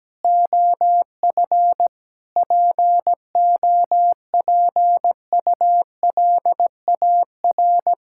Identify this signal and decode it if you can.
Morse code, heard as a keyed tone, about 17 wpm, 700 Hz. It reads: OF POPULAR
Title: Sentences from Top 1000 Words - Review Part 5 - 17wpm